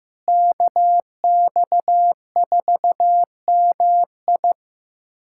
Morse code, heard as a keyed tone, about 15 wpm, 700 Hz. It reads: KX4MI
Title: K1USN SST QSO Practice - 15wpm